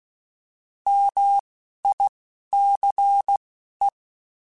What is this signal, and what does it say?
Morse code, heard as a keyed tone, about 16 wpm, spaced as for 11 wpm, 800 Hz. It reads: MICE
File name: Alphabetical / Alphabetical mp3s / Ms @10